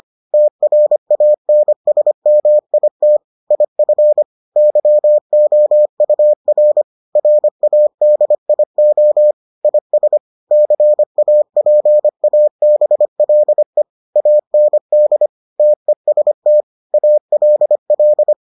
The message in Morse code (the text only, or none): TRANSMIT IF YOUR RADIO IS CAPABLE AND TEST ALL